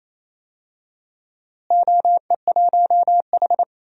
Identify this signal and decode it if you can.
Morse code, heard as a keyed tone, about 28 wpm, 700 Hz. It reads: OE1H
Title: DXCC-100 Call Signs - 4 Characters - Repeated 3x - 28wpm